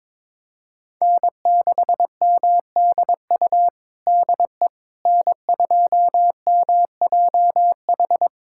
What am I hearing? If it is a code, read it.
Morse code, heard as a keyed tone, about 22 wpm, 700 Hz. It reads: N6MDU DE N2MJH
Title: USA Callsign DE Callsign - 5 Characters - 22wpm